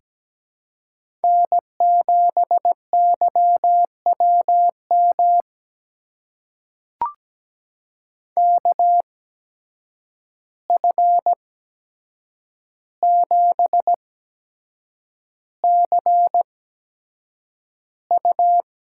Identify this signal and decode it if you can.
Morse code, heard as a keyed tone, about 17 wpm, spaced as for 7 wpm, 700 Hz. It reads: N7YWM KF7CU
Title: USA Call Signs - 5 Characters - 8x Character Spacing - 17wpm